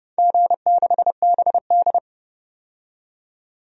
Morse code, heard as a keyed tone, about 30 wpm, 700 Hz. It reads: G6BD
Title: DXCC-100 Call Signs - 4 Characters - 30wpm